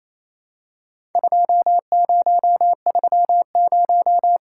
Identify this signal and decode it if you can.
Morse code, heard as a keyed tone, about 28 wpm, 700 Hz. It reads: 2030
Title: Worked All Europe DX Contest - QTC Exchange - 28wpm